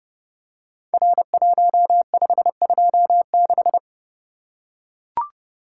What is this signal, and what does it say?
Morse code, heard as a keyed tone, about 29 wpm, 700 Hz. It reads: R1526
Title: Code Groups - 5 Letter:Numbers - 30wpm